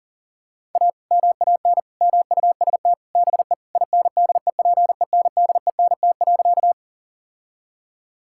Morse code, heard as a keyed tone, about 40 wpm, 700 Hz. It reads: A MAN MUST BE INDEPENDENT.
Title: Sentences from Top 1000 Words - Review Part 6 - 40wpm